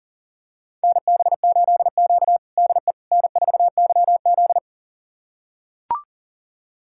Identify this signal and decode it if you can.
Morse code, heard as a keyed tone, about 40 wpm, 700 Hz. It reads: NB8Q DE N4YZ
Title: USA Callsign DE Callsign - 4 Characters - 40wpm